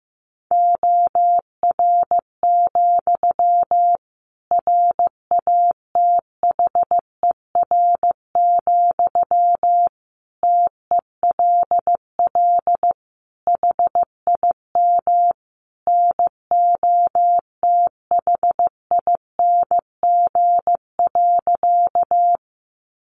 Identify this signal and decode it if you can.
Morse code, heard as a keyed tone, about 15 wpm, 700 Hz. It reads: OR, RATHER, TELL HIM NOTHING.
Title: Sentences from Top 700 Words 15wpm